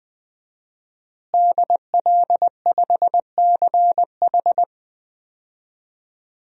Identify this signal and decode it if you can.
Morse code, heard as a keyed tone, about 20 wpm, 700 Hz. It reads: DL5CH
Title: DXCC-100 Call Signs - 5 Characters - Repeated 3x - 20wpm